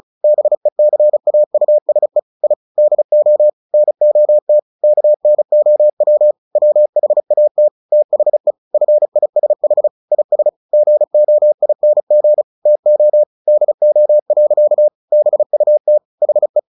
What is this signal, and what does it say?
Morse code, heard as a keyed tone, about 34 wpm, 600 Hz. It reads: BECAUSE I DO NOT KNOW WHAT THE FISH IS GOING TO DO. BUT HE
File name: the old man and the sea / the old man and the sea 1 35wpm